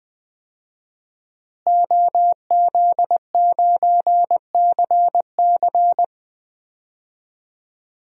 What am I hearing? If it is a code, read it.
Morse code, heard as a keyed tone, about 20 wpm, 700 Hz. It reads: OZ9CC
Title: DXCC-100 Call Signs - 5 Characters - Repeated 2x - 20wpm